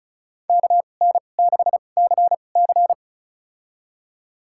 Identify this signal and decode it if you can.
Morse code, heard as a keyed tone, about 36 wpm, spaced as for 25 wpm, 700 Hz. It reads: KN6CC
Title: USA Call Signs - 5 Characters - 2x Character Spacing - 35wpm